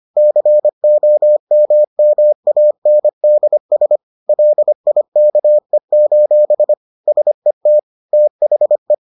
Morse code, heard as a keyed tone, about 25 wpm, 600 Hz. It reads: COMMANDS LIKE: SET THE